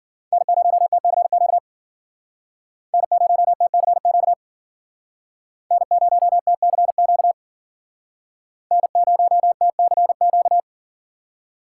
Morse code, heard as a keyed tone, about 51 wpm, 700 Hz. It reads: N0TCQ N0TCQ N0TCQ N0TCQ